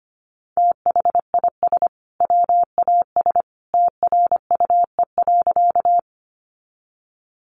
Morse code, heard as a keyed tone, about 25 wpm, 700 Hz. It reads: THIS WAS TRUE.
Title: Sentences from Top 400 Words 25wpm